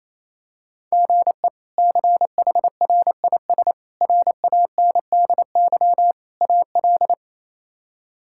Morse code, heard as a keyed tone, about 27 wpm, 700 Hz. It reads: GE CHRIS RANDY AL
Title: K1USN SST QSO Practice - 28wpm